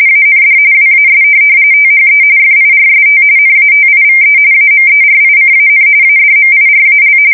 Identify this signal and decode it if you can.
RTTY, about 45.45 baud. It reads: AND WAS THE FIRST NATIVE AMERICAN ADMIRAL
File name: fsk170 / USS-New-Jersey